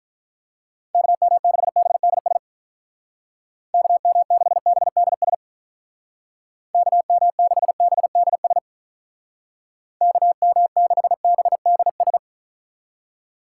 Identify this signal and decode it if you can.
Morse code, heard as a keyed tone, about 44 wpm, 700 Hz. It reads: KM6BDS KM6BDS KM6BDS KM6BDS
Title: Speed Racing - USA 2x3 Call Signs 35wpm